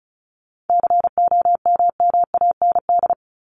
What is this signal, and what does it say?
Morse code, heard as a keyed tone, about 34 wpm, 700 Hz. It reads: COMMAND
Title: Top 1501-1750 Words 35wpm